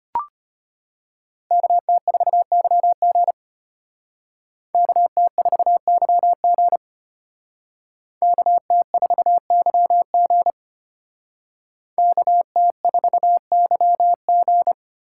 Morse code, heard as a keyed tone, about 31 wpm, 700 Hz. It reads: KT4YG KT4YG KT4YG KT4YG
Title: Speed Racing - USA 2x2 Call Signs 25wpm